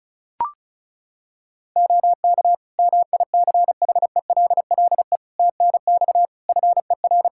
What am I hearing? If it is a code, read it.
Morse code, heard as a keyed tone, about 34 wpm, 700 Hz. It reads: OK MICHELLE TNX FER
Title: Practice QSO - Exchange 4 - Ending - 35wpm